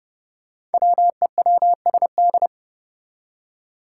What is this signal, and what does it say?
Morse code, heard as a keyed tone, about 30 wpm, 700 Hz. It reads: WEWSD
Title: Five Challenging Letter-Number Combinations 30wpm